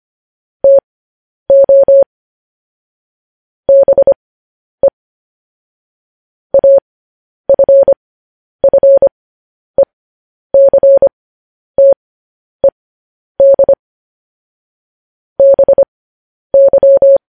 Morse code, heard as a keyed tone, about 25 wpm, spaced as for 10 wpm, 550 Hz. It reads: TO BE AFFECTED BY
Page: news-headlines-in-morse-code-at-10-wpm